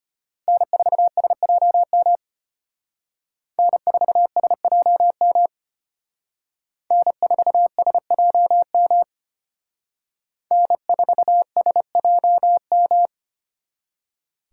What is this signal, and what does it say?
Morse code, heard as a keyed tone, about 33 wpm, 700 Hz. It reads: N4SJM N4SJM N4SJM N4SJM